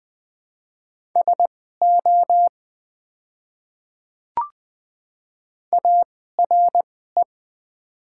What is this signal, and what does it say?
Morse code, heard as a keyed tone, about 20 wpm, spaced as for 15 wpm, 700 Hz. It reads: SO ARE
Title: 2x Character Spacing - Top 300 Words - 20wpm